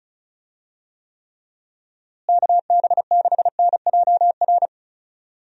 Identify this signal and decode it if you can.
Morse code, heard as a keyed tone, about 34 wpm, 700 Hz. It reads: KB6NJR
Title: USA Call Signs - 6 Characters - Repeated 3x - 35wpm